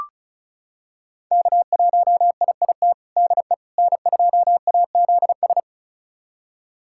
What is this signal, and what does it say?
Morse code, heard as a keyed tone, about 36 wpm, 700 Hz. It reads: K1IIT DE N2AZS
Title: USA Callsign DE Callsign - 5 Characters - 35wpm